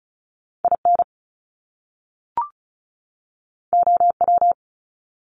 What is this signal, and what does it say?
Morse code, heard as a keyed tone, about 36 wpm, 700 Hz. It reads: IN OW
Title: Top Two Letter Combinations in Words 35wpm